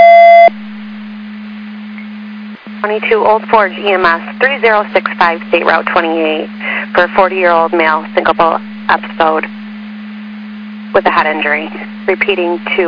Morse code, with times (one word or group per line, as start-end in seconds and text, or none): none